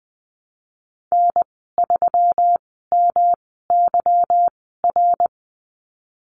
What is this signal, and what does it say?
Morse code, heard as a keyed tone, about 20 wpm, spaced as for 15 wpm, 700 Hz. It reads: N3MYR
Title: 179 Lesson 29g - Slash - Call Signs 20wpm